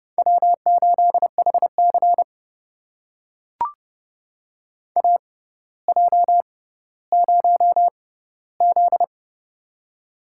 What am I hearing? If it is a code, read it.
Morse code, heard as a keyed tone, about 30 wpm, spaced as for 15 wpm, 700 Hz. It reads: W8HC AJ0Z
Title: USA Call Signs - 4 Characters - 6x Character Spacing - 30wpm